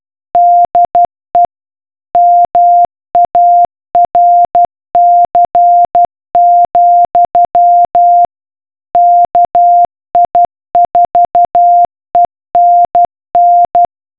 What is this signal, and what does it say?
Morse code, heard as a keyed tone, about 12 wpm, 700 Hz. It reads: DE MARC, KI4ENN